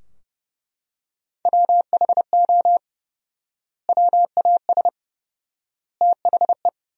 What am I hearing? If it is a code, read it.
Morse code, heard as a keyed tone, about 31 wpm, 700 Hz. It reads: WHO WAS THE
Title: 4x Word-Spacing - Sentences from Top 300 Words - 30wpm